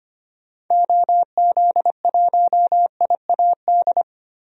Morse code, heard as a keyed tone, about 25 wpm, 700 Hz. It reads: OZ1IAD
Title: DXCC-100 Call Signs - 6 Characters - Repeated 2x - 25wpm